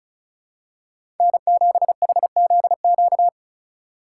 Morse code, heard as a keyed tone, about 34 wpm, 700 Hz. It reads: N7HZQ
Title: USA Call Signs - 5 Characters - 3x Character Spacing - 35wpm